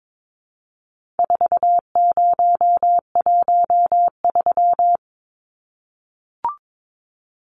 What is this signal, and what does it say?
Morse code, heard as a keyed tone, about 22 wpm, 700 Hz. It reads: 4013
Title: Numbers - 4-Digit 22wpm